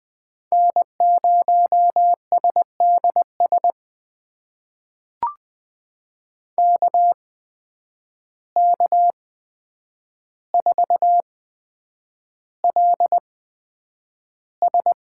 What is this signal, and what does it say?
Morse code, heard as a keyed tone, about 20 wpm, spaced as for 9 wpm, 700 Hz. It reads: N0SDS KK4LS
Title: USA Call Signs - 5 Characters - 8x Character Spacing - 20wpm